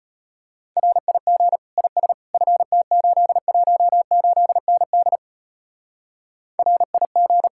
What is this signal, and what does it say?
Morse code, heard as a keyed tone, about 32 wpm, 700 Hz. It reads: RIG IS FT818ND RIG